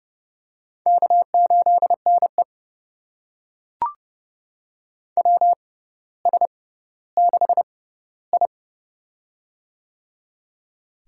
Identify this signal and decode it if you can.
Morse code, heard as a keyed tone, about 30 wpm, spaced as for 15 wpm, 700 Hz. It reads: K8NE WS6I
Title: USA Call Signs - 4 Characters - 6x Character Spacing - 30wpm